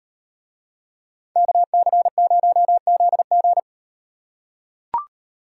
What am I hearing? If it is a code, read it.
Morse code, heard as a keyed tone, about 35 wpm, 700 Hz. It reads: KC0ZG